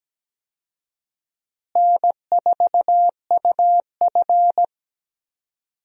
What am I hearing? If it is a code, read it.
Morse code, heard as a keyed tone, about 17 wpm, 700 Hz. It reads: N4UF
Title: USA Call Signs - 4 Characters - Repeated 2x - 17wpm